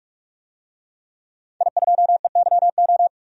Morse code, heard as a keyed tone, about 46 wpm, 700 Hz. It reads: I2EYO
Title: DXCC-100 Call Signs - 5 Characters - 45wpm